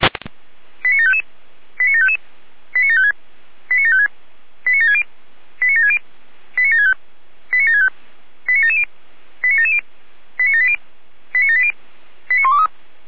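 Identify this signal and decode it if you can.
Five-tone selective call (ZVEI), heard as quick runs of five tones, tones of about 70 ms. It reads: 89755 89755 89756 89756 89760 89760 89765 89765 89770 89770 89780 89780 89123
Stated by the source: ZVEI five-tone selective call 89705, received by SDR.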